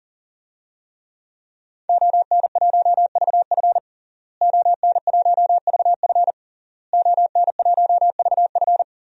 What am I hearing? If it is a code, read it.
Morse code, heard as a keyed tone, about 40 wpm, 700 Hz. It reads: ON1VF ON1VF ON1VF
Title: DXCC-100 Call Signs - 5 Characters - Repeated 3x - 40wpm